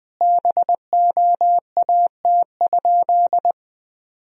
Morse code, heard as a keyed tone, about 21 wpm, 700 Hz. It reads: BOAT?
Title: Sentences from Top 1000 Words - Review Part 3 - 20wpm